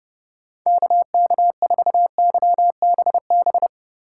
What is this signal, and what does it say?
Morse code, heard as a keyed tone, about 29 wpm, 700 Hz. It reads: KK4YBB